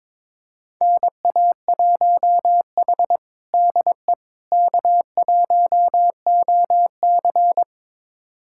Morse code, heard as a keyed tone, about 21 wpm, 700 Hz. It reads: NA1H DE K1OC
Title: USA Callsign DE Callsign - 4 Characters - 22wpm